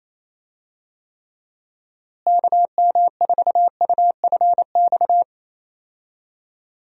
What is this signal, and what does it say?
Morse code, heard as a keyed tone, about 28 wpm, 700 Hz. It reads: KM4UFX
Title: USA Call Signs - 6 Characters - Repeated 2x - 28wpm